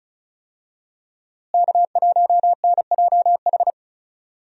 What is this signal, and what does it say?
Morse code, heard as a keyed tone, about 36 wpm, 700 Hz. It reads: K1NJH
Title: USA Call Signs - 5 Characters - Repeated 3x - 35wpm